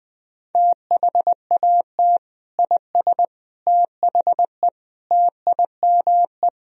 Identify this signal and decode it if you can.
Morse code, heard as a keyed tone, about 21 wpm, 700 Hz. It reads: THAT IS THE TIME
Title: Sentences from Top 1000 Words - Review Part 1 - 20wpm